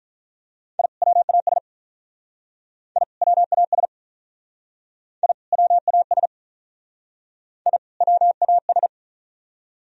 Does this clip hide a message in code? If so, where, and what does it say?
Yes: Morse code from 0.8 to 8.9 s, heard as a keyed tone, about 44 wpm, 700 Hz. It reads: I WAS I WAS I WAS I WAS